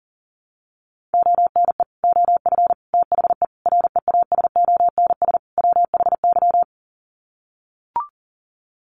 Morse code, heard as a keyed tone, about 40 wpm, 700 Hz. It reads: ONE OF THE REASONS WHY
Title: Sets of 5 Words 40wpm